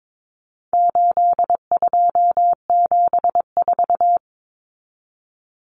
Morse code, heard as a keyed tone, about 22 wpm, 700 Hz. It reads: 8274
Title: Numbers - 4-Digit 22wpm